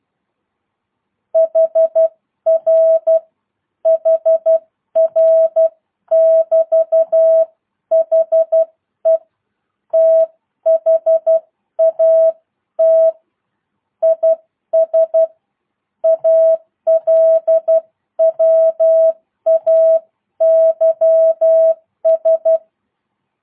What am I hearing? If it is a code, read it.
Morse code, heard as a keyed tone, about 12 wpm, 650 Hz. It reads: HR HR <BT> HE THAT IS ALWAYS